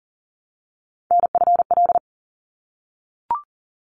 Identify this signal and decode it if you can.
Morse code, heard as a keyed tone, about 40 wpm, 700 Hz. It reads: NFL
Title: ARRL Sections 40wpm